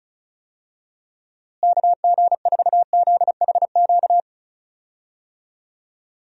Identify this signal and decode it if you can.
Morse code, heard as a keyed tone, about 36 wpm, 700 Hz. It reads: KG4ZHQ
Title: USA Call Signs - 6 Characters - Repeated 3x - 35wpm